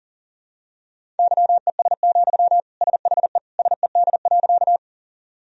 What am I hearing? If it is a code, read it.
Morse code, heard as a keyed tone, about 40 wpm, 700 Hz. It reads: YES, SHE SED.
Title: Sentences from Top 500 Words and Abbreviations 40wpm